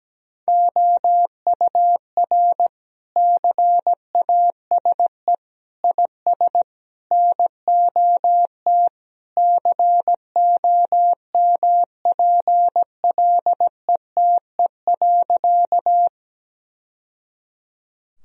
Morse code, heard as a keyed tone, about 17 wpm, 700 Hz. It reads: OUR CASE IS NOT COMPLETE.
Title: Sentences from Top 1000 Words - Review Part 2 - 17wpm